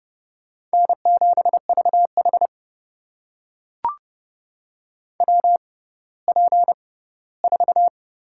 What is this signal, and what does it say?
Morse code, heard as a keyed tone, about 32 wpm, spaced as for 17 wpm, 700 Hz. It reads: N7VH WP4
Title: USA Call Signs - 6x Character Spacing - 30wpm